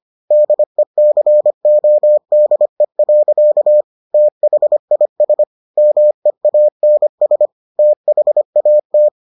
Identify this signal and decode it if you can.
Morse code, heard as a keyed tone, about 25 wpm, 600 Hz. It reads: DECODE. THIS MEANS THAT